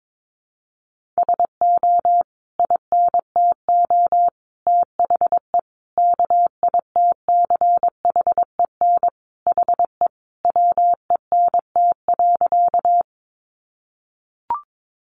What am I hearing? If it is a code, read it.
Morse code, heard as a keyed tone, about 22 wpm, 700 Hz. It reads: SO INTO THE KITCHEN HE WENT.
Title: Sentences from Top 1400 Words 22wpm